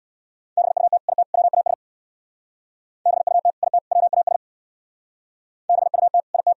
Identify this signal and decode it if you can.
Morse code, heard as a keyed tone, about 59 wpm, 700 Hz. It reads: BUT IT WAS BUT IT WAS BUT IT
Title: Speed-Racing - Sets of 3 Words 50wpm